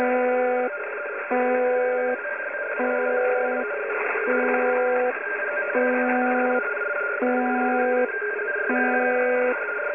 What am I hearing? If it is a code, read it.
Two-tone teleprinter signal, about 170 Hz shift.
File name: uploader / websdr recording 201